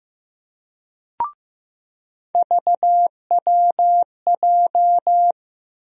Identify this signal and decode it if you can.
Morse code, heard as a keyed tone, about 15 wpm, 700 Hz. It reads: VWJ
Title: USA Call Sign Postfixes - 3 Letters 15wpm